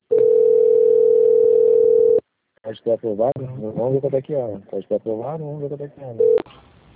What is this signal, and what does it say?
Ringback tone (North America), 440 + 480 Hz, with speech.
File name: out- - - - - .16028